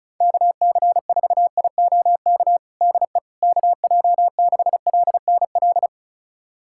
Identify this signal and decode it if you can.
Morse code, heard as a keyed tone, about 36 wpm, 700 Hz. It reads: KC4IOK DE KJ6LNL
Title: USA Callsign DE Callsign - 6 Characters - 35wpm